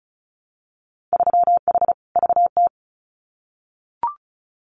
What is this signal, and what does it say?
Morse code, heard as a keyed tone, about 36 wpm, 700 Hz. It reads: 3H VT
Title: Winter Field Day Exchanges Encore 35wpm